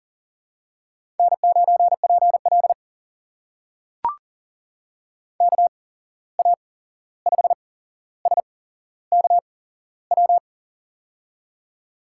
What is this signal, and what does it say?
Morse code, heard as a keyed tone, about 40 wpm, spaced as for 15 wpm, 700 Hz. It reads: N9PL KA5SKW
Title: USA Call Signs - 8x Character Spacing - 40wpm